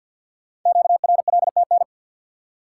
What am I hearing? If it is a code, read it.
Morse code, heard as a keyed tone, about 48 wpm, 700 Hz. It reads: QRLTN
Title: Code Groups - 5 Letters - 50wpm